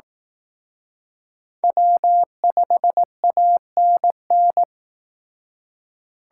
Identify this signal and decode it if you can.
Morse code, heard as a keyed tone, about 18 wpm, 700 Hz. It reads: W5ANN
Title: Speed Racing - USA 1x3 Call Signs 15wpm